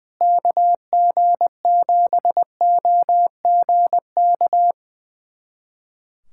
Morse code, heard as a keyed tone, about 21 wpm, 700 Hz. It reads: KG7OGK
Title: USA Callsign DE Callsign 20wpm